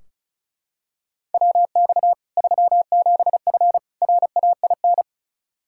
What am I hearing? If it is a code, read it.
Morse code, heard as a keyed tone, about 36 wpm, 700 Hz. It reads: WX 37F RAIN
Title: Practice QSO - Exchange 2 - Info - 35wpm